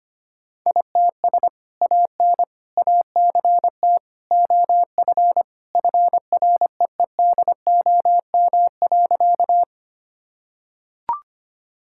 Morse code, heard as a keyed tone, about 25 wpm, 700 Hz. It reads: ITS AN ACT OF FREEDOM.
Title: Sentences from Top 1000 Words - Review Part 5 - 25wpm